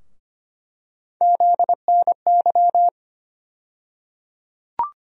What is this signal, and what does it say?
Morse code, heard as a keyed tone, about 25 wpm, 700 Hz. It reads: ZNY
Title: USA Call Sign Postfixes - 3 Letters 25wpm